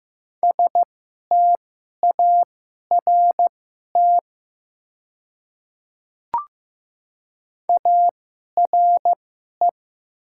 Morse code, heard as a keyed tone, about 15 wpm, spaced as for 11 wpm, 700 Hz. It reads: START ARE
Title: 2x Character Spacing - Top 300 Words - 15wpm